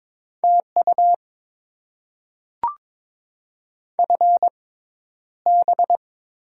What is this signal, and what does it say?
Morse code, heard as a keyed tone, about 22 wpm, spaced as for 11 wpm, 700 Hz. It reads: TU FB
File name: Top 20 QSO Elements - 6x Character Spacing - 22wpm